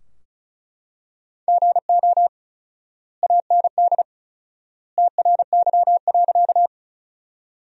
Morse code, heard as a keyed tone, about 36 wpm, 700 Hz. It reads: GO AND TRY.
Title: 4x Word-Spacing - Sentences from Top 200 Words - 35wpm